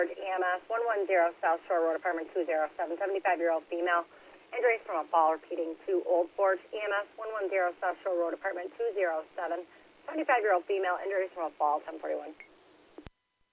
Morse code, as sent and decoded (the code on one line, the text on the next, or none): none